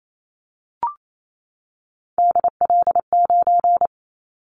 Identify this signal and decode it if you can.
Morse code, heard as a keyed tone, about 28 wpm, 700 Hz. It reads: DL9
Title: Callsign Prefixes for Top 25 DXCC - 28wpm